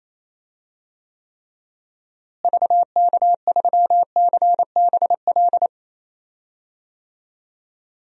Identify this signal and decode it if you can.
Morse code, heard as a keyed tone, about 28 wpm, 700 Hz. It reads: VK3CBL
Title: DXCC-100 Call Signs - 6 Characters - Repeated 2x - 28wpm